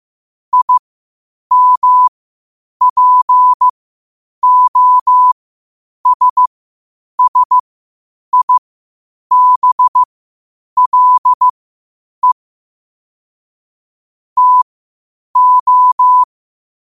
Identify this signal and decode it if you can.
Morse code, heard as a keyed tone, about 15 wpm, spaced as for 8 wpm, 1000 Hz. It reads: IMPOSSIBLE TO
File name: qotdfiles / 8wpm